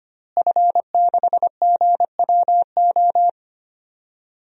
Morse code, heard as a keyed tone, about 25 wpm, 700 Hz. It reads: F6GWO
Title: DXCC-100 Call Signs - 5 Characters - Repeated 3x - 25wpm